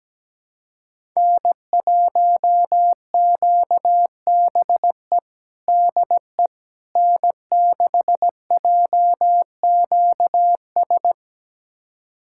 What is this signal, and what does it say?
Morse code, heard as a keyed tone, about 17 wpm, 700 Hz. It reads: N1QBE DE N6JQS
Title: USA Callsign DE Callsign - 5 Characters - 17wpm